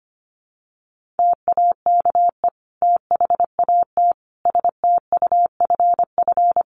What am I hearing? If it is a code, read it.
Morse code, heard as a keyed tone, about 25 wpm, 700 Hz. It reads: TAKE THAT STUFF
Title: Sentences from Top 1400 Words 25wpm